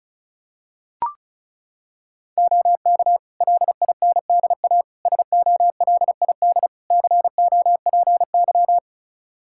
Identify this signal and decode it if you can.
Morse code, heard as a keyed tone, about 36 wpm, 700 Hz. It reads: OK LINDA SOLID COPY
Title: Practice QSO - Exchange 3 - More Info - 35wpm